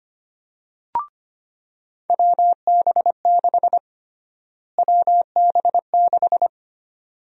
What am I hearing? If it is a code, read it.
Morse code, heard as a keyed tone, about 25 wpm, 700 Hz. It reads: WB6 WB6
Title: USA Call Sign Prefixes - Repeated 2x - 25wpm